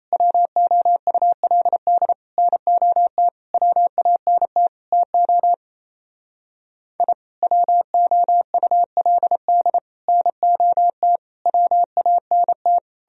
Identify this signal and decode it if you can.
Morse code, heard as a keyed tone, about 34 wpm, 700 Hz. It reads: WOULD NOT WANT TO I WOULD NOT WANT
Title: Speed-Racing - Sets of 5 Words 28wpm